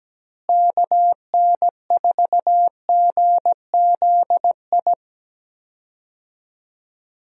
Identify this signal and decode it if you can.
Morse code, heard as a keyed tone, about 17 wpm, 700 Hz. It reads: KN4GZI
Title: USA Call Signs - 6 Characters - Repeated 3x - 17wpm